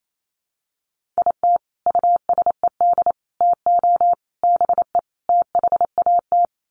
Morse code, heard as a keyed tone, about 28 wpm, 700 Hz. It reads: IT USED TO BE THAT
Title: Sets of 5 Words Encore2 28wpm